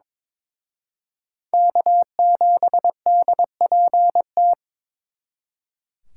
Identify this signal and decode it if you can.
Morse code, heard as a keyed tone, about 26 wpm, 700 Hz. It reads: K7DPT
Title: Speed Racing - USA 1x3 Call Signs 22wpm